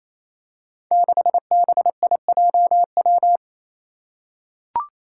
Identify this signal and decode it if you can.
Morse code, heard as a keyed tone, about 27 wpm, 700 Hz. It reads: 6BIJW
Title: Five Challenging Letter-Number Combinations 28wpm